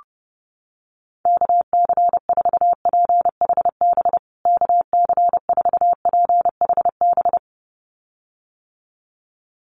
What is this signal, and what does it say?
Morse code, heard as a keyed tone, about 30 wpm, 700 Hz. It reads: KC4PHB KC4PHB
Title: USA Call Signs Repeated 2x 30wpm